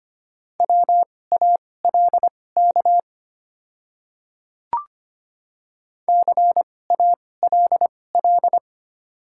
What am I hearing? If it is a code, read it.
Morse code, heard as a keyed tone, about 25 wpm, spaced as for 18 wpm, 690 Hz. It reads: WALK CALL
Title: 2x Character Spacing - Top 300 Words - 25wpm